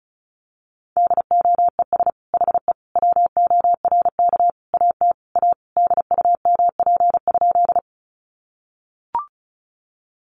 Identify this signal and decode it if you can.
Morse code, heard as a keyed tone, about 34 wpm, 700 Hz. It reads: DOES HE WORK AT A DUMP?
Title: Sentences from Top 1300 Words 35wpm